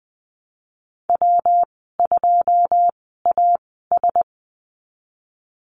Morse code, heard as a keyed tone, about 20 wpm, spaced as for 15 wpm, 700 Hz. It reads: W2AS
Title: 128 Lesson 21g - Number 6 - Call Signs 20wpm